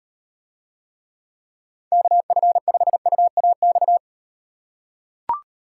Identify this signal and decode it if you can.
Morse code, heard as a keyed tone, about 38 wpm, 700 Hz. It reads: KF5UAX